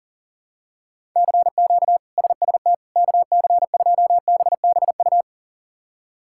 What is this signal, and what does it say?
Morse code, heard as a keyed tone, about 40 wpm, 700 Hz. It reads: CQ SST KC2BBU
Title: K1USN SST QSO Practice - 40wpm